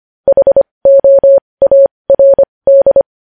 Morse code, heard as a keyed tone, about 25 wpm, spaced as for 20 wpm, 550 Hz. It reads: HOARD
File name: podcast / News headlines-1x-25x20